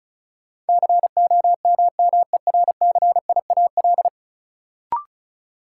Morse code, heard as a keyed tone, about 36 wpm, 700 Hz. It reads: COMMERCIAL
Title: Sentences from Top 1000 Words - Review Part 5 - New Word Drill - 35wpm